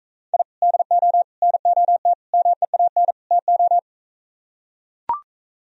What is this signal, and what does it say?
Morse code, heard as a keyed tone, about 44 wpm, 700 Hz. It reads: I DO NOT MEAN TO